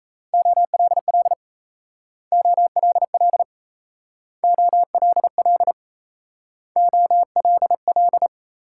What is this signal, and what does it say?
Morse code, heard as a keyed tone, about 36 wpm, 700 Hz. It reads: OLL OLL OLL OLL